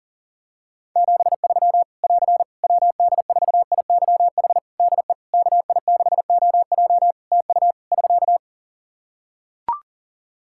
Morse code, heard as a keyed tone, about 40 wpm, 700 Hz. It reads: 73 <AR> WD4IYH DE KI6OJ TU <SK>
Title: K1USN SST QSO Practice 40wpm